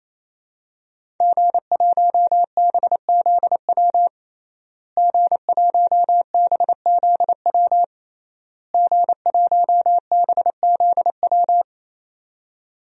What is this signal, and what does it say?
Morse code, heard as a keyed tone, about 28 wpm, 700 Hz. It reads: G1BZW G1BZW G1BZW
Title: DXCC-100 Call Signs - 5 Characters - Repeated 3x - 28wpm